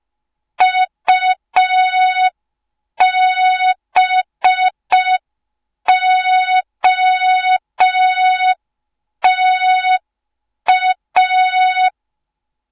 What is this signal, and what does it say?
Morse code, heard as a keyed tone, about 5 wpm, 3700 Hz, with noise. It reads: UBOTA
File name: Morze-HA2MN / hang / 06BDU HA2MN